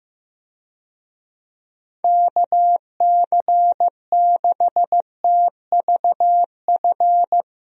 Morse code, heard as a keyed tone, about 15 wpm, 700 Hz. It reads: KC6TVF
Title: USA Call Signs - 6 Characters - Repeated 2x - 15wpm